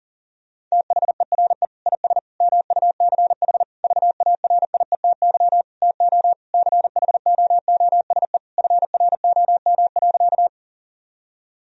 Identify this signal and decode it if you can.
Morse code, heard as a keyed tone, about 40 wpm, 700 Hz. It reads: THERE IS MUCH VARIETY TO CHOOSE FROM.
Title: Sentences from Top 1000 Words - Review Part 4 - 40wpm